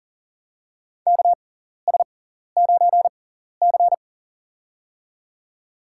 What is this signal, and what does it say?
Morse code, heard as a keyed tone, about 37 wpm, spaced as for 14 wpm, 700 Hz. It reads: KS9C
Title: USA Call Signs - 4 Characters - 6x Character Spacing - 40wpm